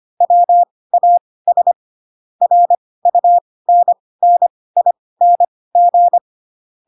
Morse code, heard as a keyed tone, about 25 wpm, spaced as for 18 wpm, 700 Hz. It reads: WAS RUNNING
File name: Pinocchio / Pinocchio ch4 12 25 ik1hge